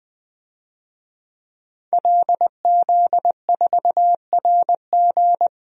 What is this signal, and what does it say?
Morse code, heard as a keyed tone, about 20 wpm, 700 Hz. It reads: LZ4RG
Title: DXCC-100 Call Signs - 5 Characters - 20wpm